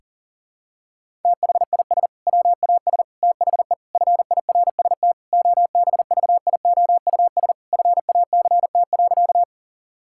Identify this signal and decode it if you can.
Morse code, heard as a keyed tone, about 37 wpm, 700 Hz. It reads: THIS WAS THE FIRST OBVIOUS FACT.
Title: Sentences from Top 1600 Words 40wpm